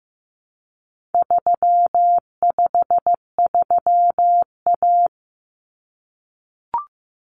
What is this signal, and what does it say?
Morse code, heard as a keyed tone, about 15 wpm, 700 Hz. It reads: 353A
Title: Common Cut Numbers - 4-Digit 15wpm